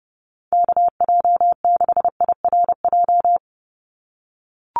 Morse code, heard as a keyed tone, about 30 wpm, 700 Hz. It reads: KJ6IRJ
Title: USA Call Signs 30wpm